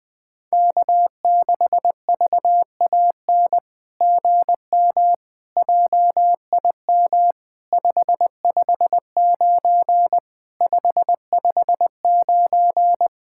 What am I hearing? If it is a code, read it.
Morse code, heard as a keyed tone, about 20 wpm, 700 Hz. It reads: K6VAN GM JIM 559 559
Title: SOTA QSO Practice 20wpm